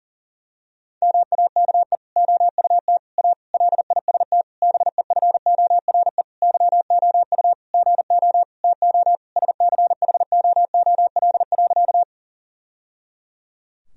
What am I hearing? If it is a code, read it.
Morse code, heard as a keyed tone, about 40 wpm, 700 Hz. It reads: MAKE OUT A LIST BEFORE YOU GO TO SCHOOL.
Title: Sentences from Top 1000 Words - Review Part 2 - 40wpm